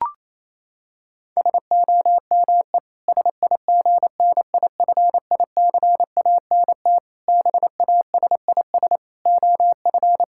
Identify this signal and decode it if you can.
Morse code, heard as a keyed tone, about 28 wpm, 700 Hz. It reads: SOME SIGNIFICANT BASIS OF
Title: Sentences from Top 1600 Words 28wpm